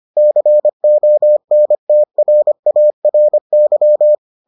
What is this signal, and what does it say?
Morse code, heard as a keyed tone, about 25 wpm, 600 Hz. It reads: CONTRARY